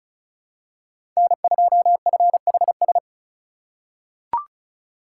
Morse code, heard as a keyed tone, about 36 wpm, 700 Hz. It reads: N2FHS
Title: USA Call Signs - 5 Characters - 2x Character Spacing - 35wpm